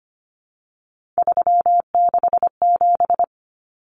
Morse code, heard as a keyed tone, about 25 wpm, 700 Hz. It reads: 367
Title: Common Cut Numbers - 3-Digit 25wpm